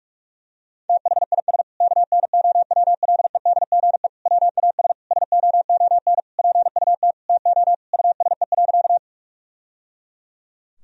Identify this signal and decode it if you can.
Morse code, heard as a keyed tone, about 41 wpm, 700 Hz. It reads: THIS KNOWLEDGE WAS SOON PUT TO USE.
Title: Sentences from Top 1000 Words - Review Part 3 - 45wpm